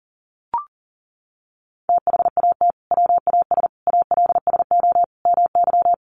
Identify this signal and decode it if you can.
Morse code, heard as a keyed tone, about 40 wpm, 700 Hz. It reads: THAT WAS ALSO MY
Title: Sentences from Top 1200 Words 40wpm